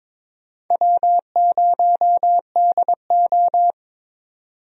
Morse code, heard as a keyed tone, about 22 wpm, 700 Hz. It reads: W0DO
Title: USA Call Signs - 4 Characters - 22wpm